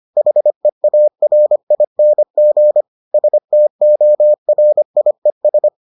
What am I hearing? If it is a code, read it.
Morse code, heard as a keyed tone, about 25 wpm, 600 Hz. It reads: HEARING STORIES